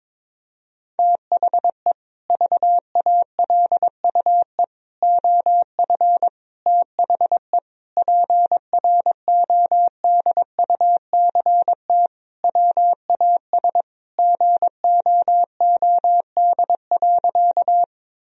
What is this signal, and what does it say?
Morse code, heard as a keyed tone, about 22 wpm, 700 Hz. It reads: THE VALUE OF THE PRODUCT WAS GOOD.
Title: Sentences from Top 1000 Words - Review Part 3 - 22wpm